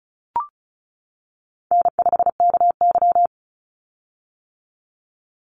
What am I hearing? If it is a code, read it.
Morse code, heard as a keyed tone, about 34 wpm, 700 Hz. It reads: N5KY
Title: Call Signs - 35wpm